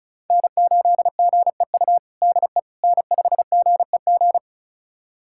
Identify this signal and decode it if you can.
Morse code, heard as a keyed tone, about 34 wpm, 700 Hz. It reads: N8GEU DE N5GEG
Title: USA Callsign DE Callsign - 5 Characters - 35wpm